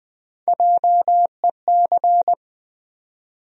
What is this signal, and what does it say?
Morse code, heard as a keyed tone, about 20 wpm, 700 Hz. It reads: JEC
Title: USA Call Sign Postfixes - 3 Letters 20wpm